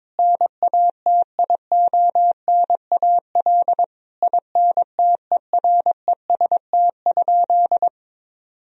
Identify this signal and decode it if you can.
Morse code, heard as a keyed tone, about 22 wpm, 700 Hz. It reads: NATIONAL INTEREST?
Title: Sentences from Top 1000 Words - Review Part 4 - 22wpm